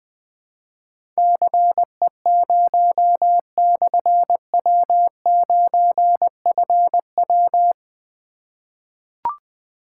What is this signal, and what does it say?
Morse code, heard as a keyed tone, about 20 wpm, 700 Hz. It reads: CE0/W9FW
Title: Difficult Call Signs 20wpm